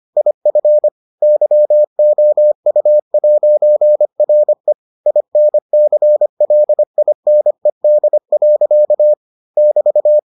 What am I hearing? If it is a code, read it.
Morse code, heard as a keyed tone, about 25 wpm, 600 Hz. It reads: IF YOU'RE INCLINED. <BT>